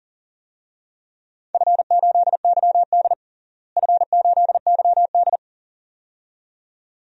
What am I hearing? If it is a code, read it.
Morse code, heard as a keyed tone, about 40 wpm, 700 Hz. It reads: F8YD F8YD
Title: DXCC-100 Call Signs - 4 Characters - Repeated 2x - 40wpm